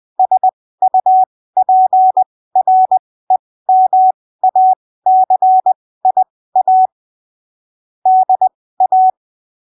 Morse code, heard as a keyed tone, about 20 wpm, spaced as for 15 wpm, 750 Hz. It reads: SUPREMACIA DA
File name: arquivos / texto / texto 24 20ppm